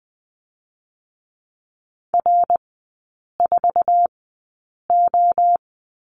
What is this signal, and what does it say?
Morse code, heard as a keyed tone, about 20 wpm, spaced as for 8 wpm, 700 Hz. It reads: R4O
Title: 060 Lesson 11e - Letter h - Three Characters 20wpm